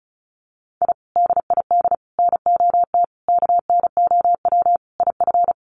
Morse code, heard as a keyed tone, about 34 wpm, 700 Hz. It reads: I DID NOT KNOW IF
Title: Sets of 5 Words Encore 35wpm